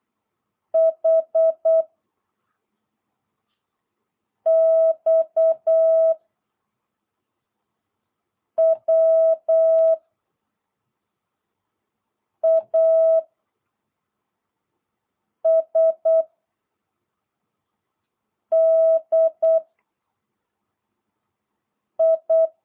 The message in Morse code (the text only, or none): H X W A S D I